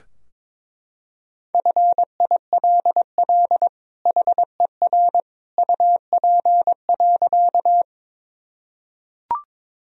Morse code, heard as a keyed tone, about 22 wpm, 700 Hz. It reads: FILL HER UP.
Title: Sentences from Top 1000 Words - Review Part 3 - 22wpm